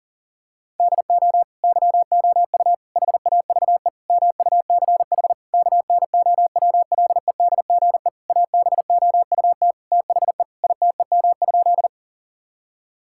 Morse code, heard as a keyed tone, about 40 wpm, 700 Hz. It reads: DO YOU HAVE MUCH KNOWLEDGE ABOUT THE ITEM?
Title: Sentences from Top 1000 Words - Review Part 6 - 40wpm